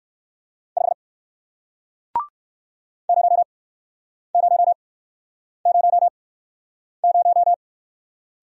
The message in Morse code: 5 0 0 0 0

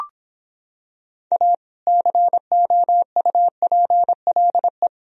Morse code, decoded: A COUPLE

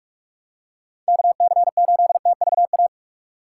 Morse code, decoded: KC8TVA

